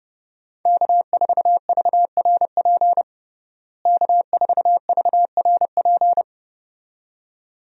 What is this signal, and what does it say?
Morse code, heard as a keyed tone, about 30 wpm, 700 Hz. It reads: K4VRP K4VRP
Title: USA Call Signs - 5 Characters - Repeated 2x - 30wpm